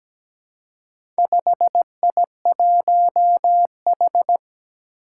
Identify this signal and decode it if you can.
Morse code, heard as a keyed tone, about 17 wpm, 700 Hz. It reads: 5I1H
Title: Four Challenging Letter-Number Combinations 17wpm